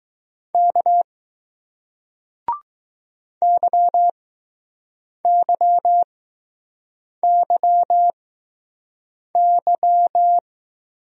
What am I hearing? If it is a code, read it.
Morse code, heard as a keyed tone, about 21 wpm, spaced as for 6 wpm, 700 Hz. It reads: K YYYY